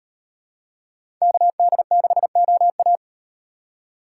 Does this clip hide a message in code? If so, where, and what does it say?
Yes: Morse code from 1.2 to 3.0 s, heard as a keyed tone, about 38 wpm, 700 Hz. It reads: KD6OA